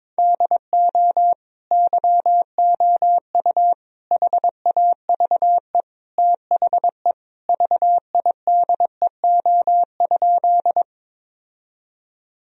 Morse code, heard as a keyed tone, about 22 wpm, 700 Hz. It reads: DO YOU HAVE THE VIDEO?
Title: Sentences from Top 1000 Words - Review Part 4 - 22wpm